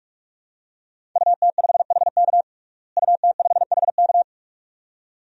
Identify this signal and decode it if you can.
Morse code, heard as a keyed tone, about 46 wpm, 700 Hz. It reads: UT5HK UT5HK
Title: DXCC-100 Call Signs - 5 Characters - Repeated 2x - 45wpm